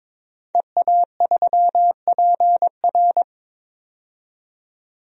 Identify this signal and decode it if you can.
Morse code, heard as a keyed tone, about 22 wpm, 700 Hz. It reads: EA3PR
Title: DXCC-100 Call Signs - 5 Characters - 22wpm